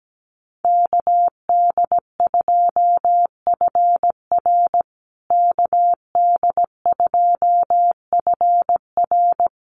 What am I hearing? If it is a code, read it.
Morse code, heard as a keyed tone, about 17 wpm, 700 Hz. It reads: KD2FR KD2FR
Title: USA Call Signs Repeated 2x 17wpm